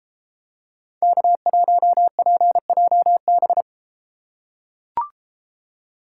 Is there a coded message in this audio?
Yes: Morse code from 1.0 to 3.6 s, heard as a keyed tone, about 33 wpm, 700 Hz. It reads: K1PJB